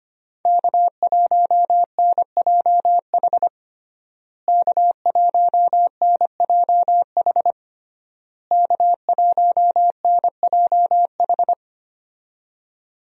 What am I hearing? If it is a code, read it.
Morse code, heard as a keyed tone, about 25 wpm, 700 Hz. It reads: K1NJH K1NJH K1NJH
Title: USA Call Signs - 5 Characters - Repeated 3x - 25wpm